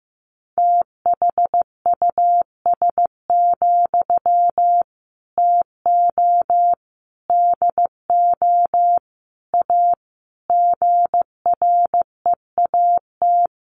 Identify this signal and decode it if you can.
Morse code, heard as a keyed tone, about 15 wpm, 700 Hz. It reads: THUS, TO DO A GREAT
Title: Sentences from Top 900 Words 15wpm